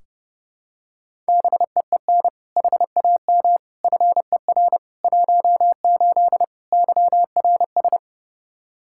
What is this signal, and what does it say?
Morse code, heard as a keyed tone, about 30 wpm, 700 Hz. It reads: BEEN HAM FER 18 YRS
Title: Practice QSO - Exchange 3 - More Info - 30wpm